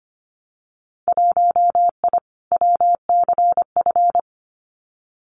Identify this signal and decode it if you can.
Morse code, heard as a keyed tone, about 25 wpm, 700 Hz. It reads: 1I WCF
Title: Winter Field Day Exchanges Encore 25wpm